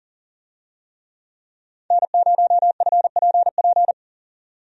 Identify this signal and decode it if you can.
Morse code, heard as a keyed tone, about 40 wpm, 700 Hz. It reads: N0FPP
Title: USA Call Signs - 5 Characters - 40wpm